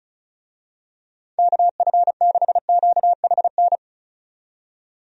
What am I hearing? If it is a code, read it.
Morse code, heard as a keyed tone, about 36 wpm, 700 Hz. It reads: KF6QHN
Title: USA Call Signs - 6 Characters - 35wpm